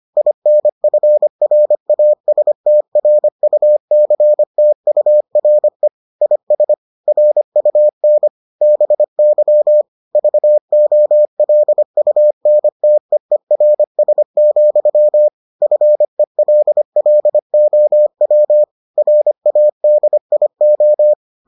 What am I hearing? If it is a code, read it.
Morse code, heard as a keyed tone, about 25 wpm, 600 Hz. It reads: INFRASTRUCTURE IS RUN BY VOLUNTEERS, FELLOW RADIO